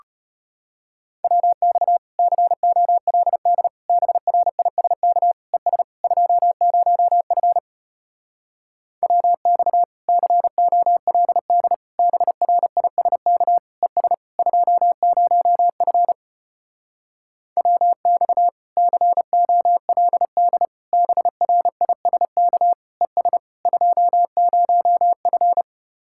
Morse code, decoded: WX COLD BRISK ES 20F WX COLD BRISK ES 20F WX COLD BRISK ES 20F

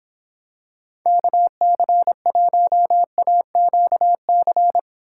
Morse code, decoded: KC1AQC

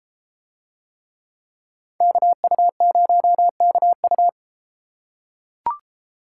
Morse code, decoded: KU0KU

